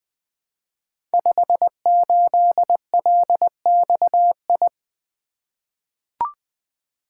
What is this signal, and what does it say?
Morse code, heard as a keyed tone, about 20 wpm, 700 Hz. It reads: 58LXI
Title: Code Groups - 5 Letter:Numbers - 20wpm